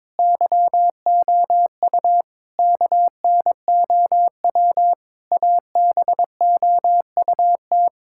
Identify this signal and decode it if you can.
Morse code, heard as a keyed tone, about 22 wpm, 700 Hz. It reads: YOU KNOW ABOUT
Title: Sentences from Top 1000 Words - Review Part 6 - 22wpm